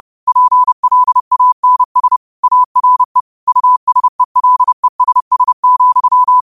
Morse code, heard as a keyed tone, about 30 wpm, 1000 Hz. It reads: PLANS ARE USELESS,